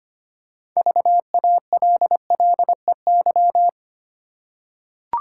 Morse code, heard as a keyed tone, about 25 wpm, 700 Hz. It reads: VALLEY
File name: practice / 25wpm / Easy Words with Hints - 25wpm